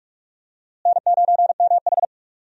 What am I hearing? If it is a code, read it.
Morse code, heard as a keyed tone, about 46 wpm, 700 Hz. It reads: N9MH
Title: USA Call Signs - 4 Characters - Repeated 3x - 45wpm